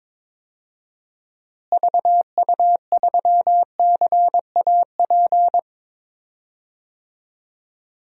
Morse code, heard as a keyed tone, about 21 wpm, 700 Hz. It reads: VU3CAP
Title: DXCC-100 Call Signs - 6 Characters - Repeated 3x - 22wpm